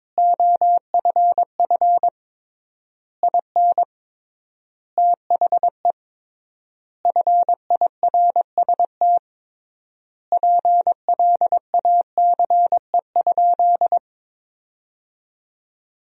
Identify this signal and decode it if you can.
Morse code, heard as a keyed tone, about 22 wpm, 700 Hz. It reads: OFF IN THE FIRST PLACE?
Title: 3x Word-Spacing - Sentences from Top 200 Words - 22wpm